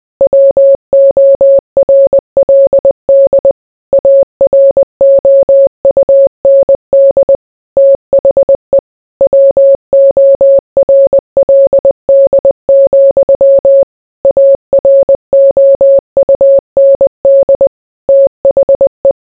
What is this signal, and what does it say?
Morse code, heard as a keyed tone, about 20 wpm, 550 Hz. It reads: WORLD AROUND THE WORLD, AROUND THE